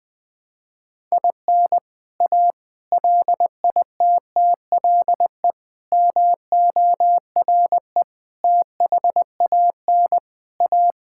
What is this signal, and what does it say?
Morse code, heard as a keyed tone, about 20 wpm, 700 Hz. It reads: IN A LITTLE MORE THAN A
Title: Sentences from Top 1000 Words - Review Part 5 - 20wpm